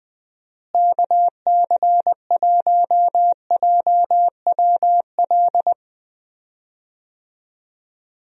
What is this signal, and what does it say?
Morse code, heard as a keyed tone, about 20 wpm, 700 Hz. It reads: KC1JWL
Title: USA Call Signs - 6 Characters - Repeated 2x - 20wpm